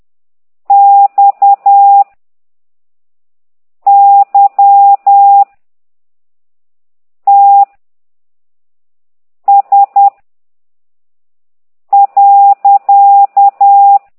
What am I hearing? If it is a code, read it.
Morse code, heard as a keyed tone, about 10 wpm, 800 Hz. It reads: X Y T S .
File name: exerc7